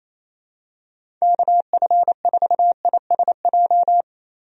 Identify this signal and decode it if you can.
Morse code, heard as a keyed tone, about 28 wpm, 700 Hz. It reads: KF4ISJ
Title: USA Call Signs - 6 Characters - Repeated 3x - 28wpm